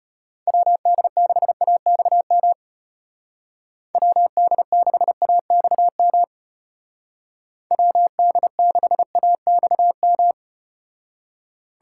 Morse code, heard as a keyed tone, about 34 wpm, 700 Hz. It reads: WD6AXM WD6AXM WD6AXM